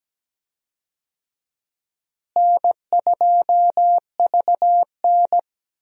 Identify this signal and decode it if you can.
Morse code, heard as a keyed tone, about 17 wpm, 700 Hz. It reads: N2VN
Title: USA Call Signs - 4 Characters - Repeated 2x - 17wpm